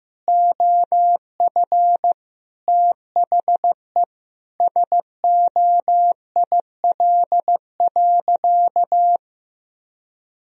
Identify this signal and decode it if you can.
Morse code, heard as a keyed tone, about 14 wpm, 700 Hz. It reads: OF THE SOIL.
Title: Sentences from Top 1000 Words - Review Part 4 - 15wpm